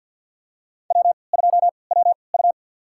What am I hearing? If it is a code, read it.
Morse code, heard as a keyed tone, about 53 wpm, spaced as for 29 wpm, 700 Hz. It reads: W2WU
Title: USA Call Signs - 3x Character Spacing - 50wpm